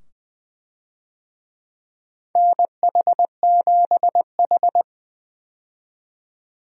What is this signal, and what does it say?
Morse code, heard as a keyed tone, about 20 wpm, 700 Hz. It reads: NH7H
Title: USA Call Signs - 4 Characters - 20wpm